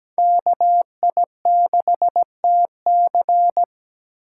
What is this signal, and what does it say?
Morse code, heard as a keyed tone, about 17 wpm, 700 Hz. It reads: KI6TC
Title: USA Call Signs - 5 Characters - 2x Character Spacing - 17wpm